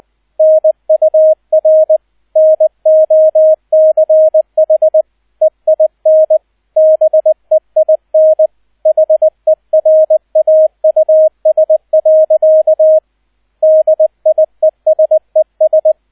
Morse code, heard as a keyed tone, about 20 wpm, 630 Hz, with radio noise. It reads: NUR NOCH EIN BEIN HERAUS. DIESES
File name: 002 Zweiter Streich 20wpm